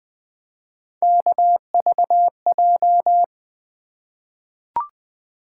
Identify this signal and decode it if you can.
Morse code, heard as a keyed tone, about 20 wpm, 700 Hz. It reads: KVJ
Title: Three Letter-Number Combinations - 20wpm